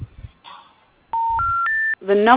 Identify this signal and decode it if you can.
Special information tone, with speech.